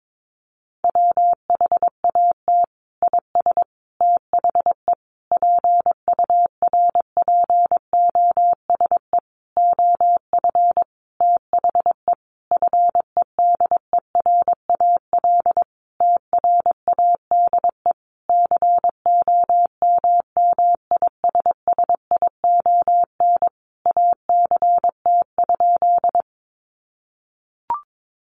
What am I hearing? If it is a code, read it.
Morse code, heard as a keyed tone, about 22 wpm, 700 Hz. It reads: WHAT IS THE PURPOSE OF THE FEDERAL TRADE COMMISSION ACT?
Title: Sentences from Top 1400 Words 22wpm